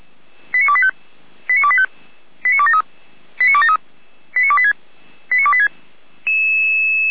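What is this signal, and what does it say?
Five-tone selective call (ZVEI), heard as quick runs of five tones, tones of about 70 ms. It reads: 89285 89285 89273 89273 89286 89286
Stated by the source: ZVEI five-tone selective call received via SDR.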